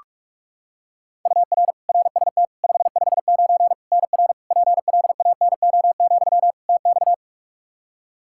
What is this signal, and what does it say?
Morse code, heard as a keyed tone, about 41 wpm, 700 Hz. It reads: UR RST 559 NR PLANO, TX
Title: Practice QSO - Exchange 1 - Beginning - 45wpm